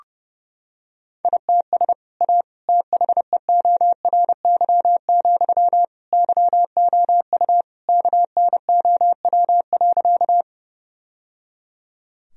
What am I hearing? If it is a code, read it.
Morse code, heard as a keyed tone, about 30 wpm, 700 Hz. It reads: ITS A THEORY, YOU KNOW.
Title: Sentences from Top 1000 Words - Review Part 3 - 30wpm